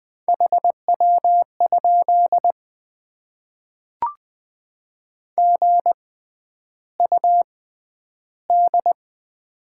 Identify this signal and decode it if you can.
Morse code, heard as a keyed tone, about 20 wpm, spaced as for 10 wpm, 700 Hz. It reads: HW? GUD
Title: Top 20 QSO Elements - 6x Character Spacing - 20wpm